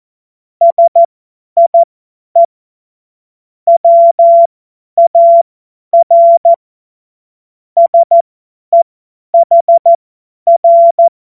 Morse code, heard as a keyed tone, about 14 wpm, spaced as for 10 wpm, 680 Hz. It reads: SIE WAR SEHR
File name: Morsekurs / Witze / Witz 05. Oktober 2021, 50 F70